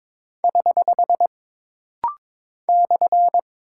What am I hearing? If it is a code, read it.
Morse code, heard as a keyed tone, about 22 wpm, 700 Hz. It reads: <HH> /
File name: practice / 22wpm / Single Punctuation-Prosign - Rapid-Fire - 22wpm